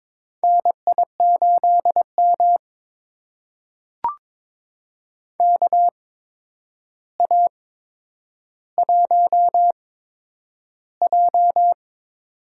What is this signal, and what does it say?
Morse code, heard as a keyed tone, about 22 wpm, spaced as for 9 wpm, 700 Hz. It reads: NI8M KA1J
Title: USA Call Signs - 8x Character Spacing - 22wpm